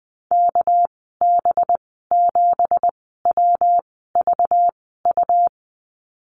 Morse code, heard as a keyed tone, about 20 wpm, spaced as for 14 wpm, 700 Hz. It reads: KB7WVU
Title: 231 Lesson 37g - Letter x - Call Signs 20wpm